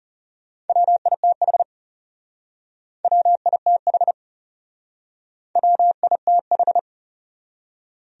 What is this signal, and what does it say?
Morse code, heard as a keyed tone, about 37 wpm, 700 Hz. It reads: WITH WITH WITH
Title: Speed-Racing Top 200 Words 30wpm